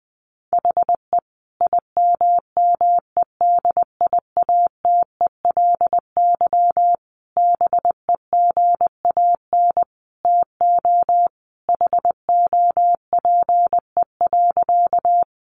Morse code, heard as a keyed tone, about 19 wpm, 700 Hz. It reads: HE IMMEDIATELY BEGAN TO HOPE.
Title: Sentences from Top 1100 Words 20wpm